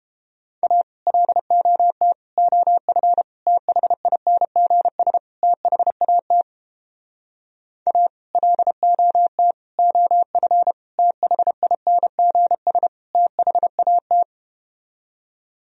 Morse code, heard as a keyed tone, about 33 wpm, 700 Hz. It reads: A LOT OF THINGS THAT A LOT OF THINGS THAT